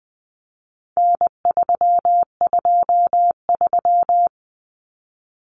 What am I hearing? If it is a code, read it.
Morse code, heard as a keyed tone, about 20 wpm, 700 Hz. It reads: N323
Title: Common Cut Numbers - 4-Digit 20wpm